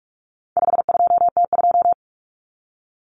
Morse code, heard as a keyed tone, about 44 wpm, 700 Hz. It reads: 52T2
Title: Common Cut Numbers - 4-Digit 45wpm